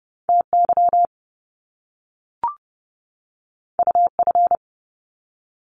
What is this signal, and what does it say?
Morse code, heard as a keyed tone, about 30 wpm, 700 Hz. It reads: TY UF
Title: Top Two Letter Combinations in Words 30wpm